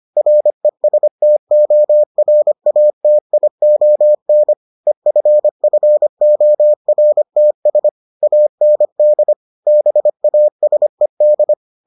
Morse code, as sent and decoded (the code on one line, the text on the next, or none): .-. ... - --- .-. .- - .. --- -. / . ..-. ..-. --- .-. - ... / .- -. -.. / -... .- ... -..
RESTORATION EFFORTS AND BASED